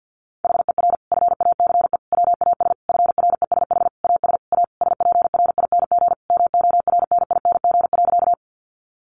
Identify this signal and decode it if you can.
Morse code, heard as a keyed tone, about 50 wpm, 700 Hz. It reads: HER FACE WAS FRESH AS A SPRING MORNING.
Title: Sentences from Top 1200 Words 50wpm